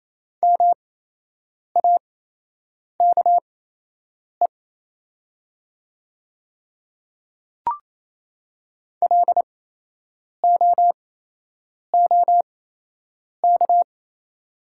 Morse code, heard as a keyed tone, about 28 wpm, spaced as for 8 wpm, 700 Hz. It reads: MAKE LOOK
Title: 8x Character Spacing - Top 100 Words - 28wpm